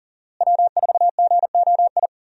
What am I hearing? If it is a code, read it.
Morse code, heard as a keyed tone, about 40 wpm, 700 Hz. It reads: W4GOI
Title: USA Call Signs - 5 Characters - Repeated 3x - 40wpm